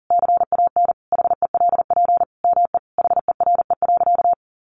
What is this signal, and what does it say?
Morse code, heard as a keyed tone, about 40 wpm, 700 Hz. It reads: CAN HELP ME HERE.
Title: Sentences from Top 200 Words 40wpm